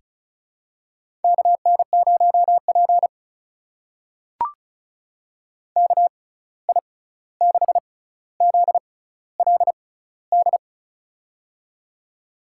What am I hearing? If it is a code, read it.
Morse code, heard as a keyed tone, about 36 wpm, spaced as for 16 wpm, 700 Hz. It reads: KN0P KI6ZLD
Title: USA Call Signs - 6x Character Spacing - 35wpm